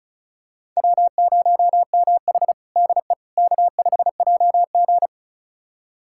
Morse code, heard as a keyed tone, about 36 wpm, 700 Hz. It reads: W0MH DE K5JG
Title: USA Callsign DE Callsign - 4 Characters - 35wpm